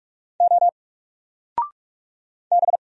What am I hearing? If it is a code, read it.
Morse code, heard as a keyed tone, about 44 wpm, 700 Hz. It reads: O B
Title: Single Letter-Number Alt 45wpm